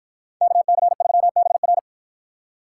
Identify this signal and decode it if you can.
Morse code, heard as a keyed tone, about 53 wpm, 700 Hz. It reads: KC3BR